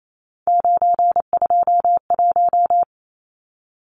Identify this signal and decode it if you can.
Morse code, heard as a keyed tone, about 28 wpm, 700 Hz. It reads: 921
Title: Numbers - 3-Digit 28wpm